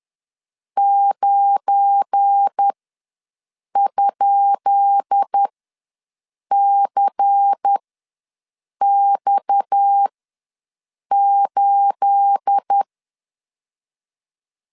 Morse code, heard as a keyed tone, about 10 wpm, spaced as for 6 wpm, 800 Hz. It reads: 9?CX8
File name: LEZIONI / 10 SIGNS / 10C DOT ws 10wpm 4213